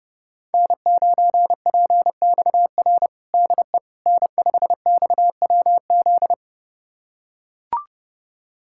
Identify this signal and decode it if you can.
Morse code, heard as a keyed tone, about 30 wpm, 700 Hz. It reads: N9PXR DE N5XWZ
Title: USA Callsign DE Callsign - 5 Characters - 30wpm